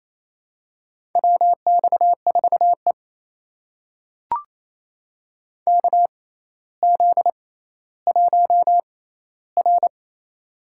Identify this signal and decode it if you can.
Morse code, heard as a keyed tone, about 28 wpm, spaced as for 14 wpm, 700 Hz. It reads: WX4E KZ1R
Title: USA Call Signs - 4 Characters - 6x Character Spacing - 28wpm